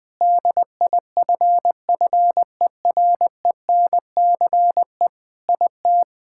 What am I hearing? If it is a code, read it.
Morse code, heard as a keyed tone, about 21 wpm, 700 Hz. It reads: DIFFERENCE IT
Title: Sentences from Top 1000 Words - Review Part 5 - 20wpm